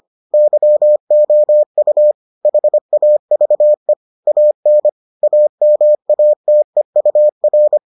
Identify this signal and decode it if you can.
Morse code, heard as a keyed tone, about 25 wpm, 600 Hz. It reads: YOU HAVE AN AMATEUR